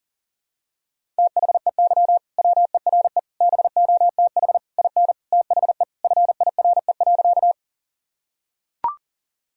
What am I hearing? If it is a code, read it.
Morse code, heard as a keyed tone, about 40 wpm, 700 Hz. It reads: THEY WERE BOTH IN THE FIRE.
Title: Sentences from Top 1000 Words - Review Part 1 - 40wpm